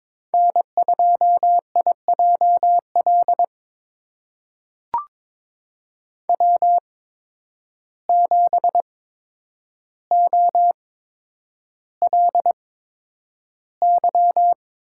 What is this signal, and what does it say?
Morse code, heard as a keyed tone, about 22 wpm, spaced as for 9 wpm, 700 Hz. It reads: N2IJL W7OLY
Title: USA Call Signs - 5 Characters - 8x Character Spacing - 22wpm